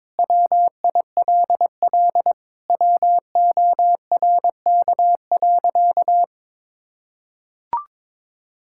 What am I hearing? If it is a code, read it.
Morse code, heard as a keyed tone, about 22 wpm, 700 Hz. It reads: WILL WORK.
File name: practice / 22wpm / Sentences with 8-Word Phrases - 22wpm